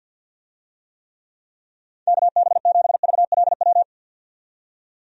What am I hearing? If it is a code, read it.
Morse code, heard as a keyed tone, about 51 wpm, 700 Hz. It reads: KB7VLW
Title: USA Call Signs - 6 Characters - 50wpm